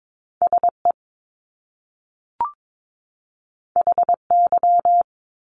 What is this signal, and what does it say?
Morse code, heard as a keyed tone, about 22 wpm, 700 Hz. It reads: SE HY
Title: Top Two Letter Combinations in Words 22wpm